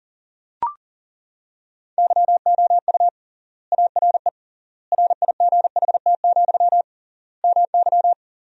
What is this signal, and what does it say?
Morse code, heard as a keyed tone, about 40 wpm, 700 Hz. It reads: YOU ARE RIGHT, MY
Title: 3x Word-Spacing - Sentences from Top 400 Words - 40wpm